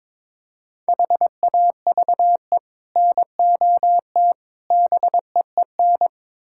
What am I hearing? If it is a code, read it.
Morse code, heard as a keyed tone, about 21 wpm, 700 Hz. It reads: HAVE NOT BEEN
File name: practice / 22wpm / Build A Phrase - 5-Word Phrases - 22wpm